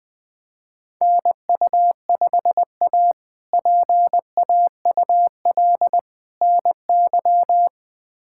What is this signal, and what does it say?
Morse code, heard as a keyed tone, about 20 wpm, 700 Hz. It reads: NU5A PAUL NY
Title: K1USN SST QSO Practice - 20wpm